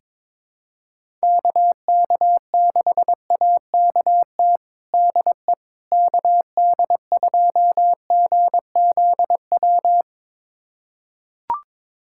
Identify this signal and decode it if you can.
Morse code, heard as a keyed tone, about 22 wpm, 700 Hz. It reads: KK6AKT DE KD2GZW
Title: USA Callsign DE Callsign - 6 Characters - 22wpm